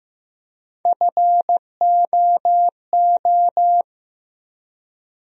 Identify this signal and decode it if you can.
Morse code, heard as a keyed tone, about 15 wpm, 700 Hz. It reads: FOO
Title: USA Call Sign Postfixes - 3 Letters 15wpm